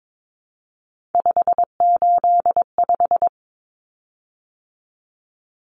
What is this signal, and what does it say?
Morse code, heard as a keyed tone, about 22 wpm, 700 Hz. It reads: 585
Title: Common Cut Numbers - 3-Digit 22wpm